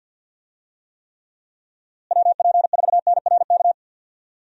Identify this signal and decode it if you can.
Morse code, heard as a keyed tone, about 52 wpm, 700 Hz. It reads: WP4NRK
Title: USA Call Signs - 6 Characters - Repeated 3x - 50wpm